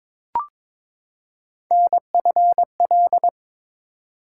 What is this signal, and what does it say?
Morse code, heard as a keyed tone, about 23 wpm, 700 Hz. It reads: NFL
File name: ARRL Sections v2 - 22wpm